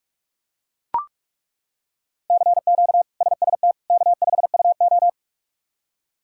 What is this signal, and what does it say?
Morse code, heard as a keyed tone, about 41 wpm, 700 Hz. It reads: CQ SST K5UO
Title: K1USN SST QSO Practice - 45wpm